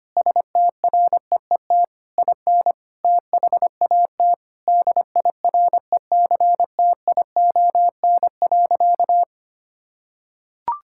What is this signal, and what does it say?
Morse code, heard as a keyed tone, about 25 wpm, 700 Hz. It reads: STREET IN THAT DIRECTION.
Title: Sentences from Top 1000 Words - Review Part 5 - 25wpm